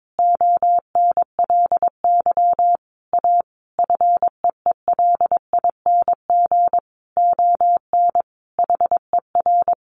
Morse code, heard as a keyed tone, about 22 wpm, 700 Hz. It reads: ONLY A FEELING ON HER
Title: Sentences from Top 1200 Words 22wpm